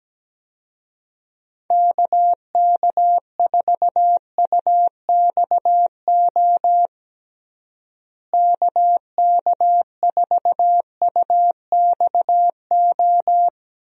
Morse code, heard as a keyed tone, about 17 wpm, 700 Hz. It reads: KK4UXO KK4UXO
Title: USA Call Signs - 6 Characters - Repeated 3x - 17wpm